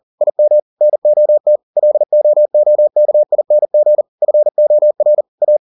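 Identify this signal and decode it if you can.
Morse code, heard as a keyed tone, about 40 wpm, 600 Hz. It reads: IM NOT LOOKING FOR A
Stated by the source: Morse code at 40 wpm, practice text, CW med Gnister.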